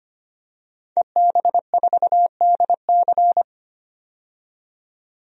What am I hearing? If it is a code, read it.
Morse code, heard as a keyed tone, about 25 wpm, 700 Hz. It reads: EB4DC
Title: DXCC-100 Call Signs - 5 Characters - 25wpm